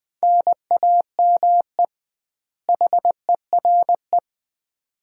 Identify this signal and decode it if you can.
Morse code, heard as a keyed tone, about 20 wpm, 700 Hz. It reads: NAME HERE
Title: Common QSO Phrases - 2x Word-Spacing - 20wpm